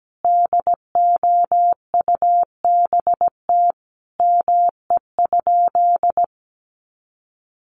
Morse code, heard as a keyed tone, about 17 wpm, 700 Hz. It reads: DOUBT ME?
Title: Sentences from Top 1500 Words 17wpm